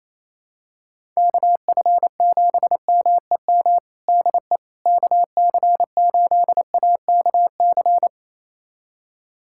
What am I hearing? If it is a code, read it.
Morse code, heard as a keyed tone, about 28 wpm, 700 Hz. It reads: KF7MEM DE KC8AKC
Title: USA Callsign DE Callsign - 6 Characters - 28wpm